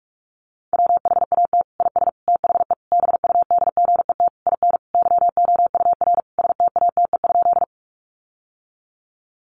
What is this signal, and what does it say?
Morse code, heard as a keyed tone, about 44 wpm, 700 Hz. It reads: WHAT IS THE BUDGET IN YOUR STATE?
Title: Sentences from Top 1200 Words 45wpm